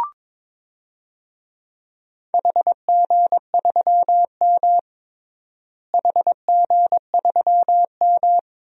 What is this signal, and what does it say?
Morse code, heard as a keyed tone, about 22 wpm, 700 Hz. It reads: HG3M HG3M
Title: DXCC-100 Call Signs - 4 Characters - Repeated 3x - 22wpm